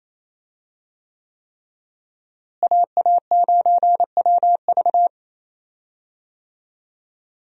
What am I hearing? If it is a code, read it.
Morse code, heard as a keyed tone, about 28 wpm, 700 Hz. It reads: AA9WV
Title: USA Call Signs - 5 Characters - Repeated 3x - 28wpm